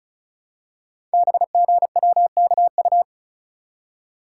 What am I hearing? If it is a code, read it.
Morse code, heard as a keyed tone, about 36 wpm, 700 Hz. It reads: BGWKU
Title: Code Groups - 5 Letters - 35wpm